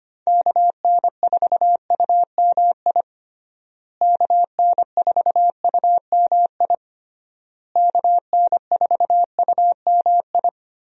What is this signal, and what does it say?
Morse code, heard as a keyed tone, about 25 wpm, 700 Hz. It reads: KN4UMI KN4UMI KN4UMI
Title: USA Call Signs - 6 Characters - Repeated 3x - 25wpm